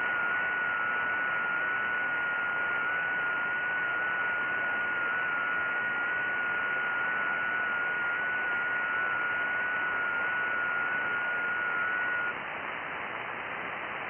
FT8.
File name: IC-7100